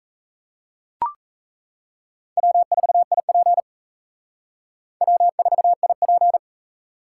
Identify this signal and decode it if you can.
Morse code, heard as a keyed tone, about 35 wpm, 700 Hz. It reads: W4IP W4IP
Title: Speed Racing - USA 1x2 Call Signs 28wpm